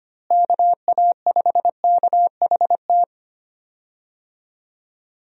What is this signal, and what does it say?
Morse code, heard as a keyed tone, about 25 wpm, 700 Hz. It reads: KA5KHT
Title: USA Call Signs - 6 Characters - Repeated 3x - 25wpm